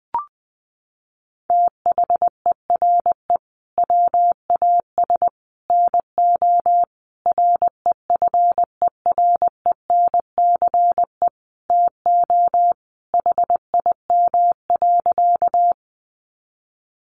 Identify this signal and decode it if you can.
Morse code, heard as a keyed tone, about 20 wpm, 700 Hz. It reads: THERE WAS NO REFERENCE TO HIM.
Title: Sentences from Top 1200 Words 20wpm